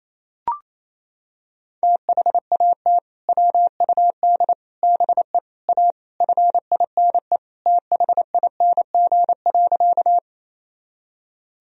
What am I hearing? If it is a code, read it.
Morse code, heard as a keyed tone, about 28 wpm, 700 Hz. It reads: THAT WUD BE A FINE THING.
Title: Sentences from Top 500 Words and Abbreviations 28wpm